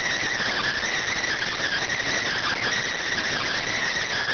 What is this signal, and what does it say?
Emergency Alert System (SAME) tones, heard as harsh buzzing bursts. It is heard from 1.0 to 3.8 s.